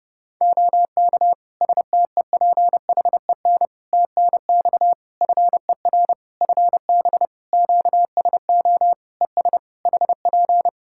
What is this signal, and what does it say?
Morse code, heard as a keyed tone, about 32 wpm, 700 Hz. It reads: OK STEPHEN TNX FER FB QSO ES HP
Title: K1USN SST QSO Practice 30wpm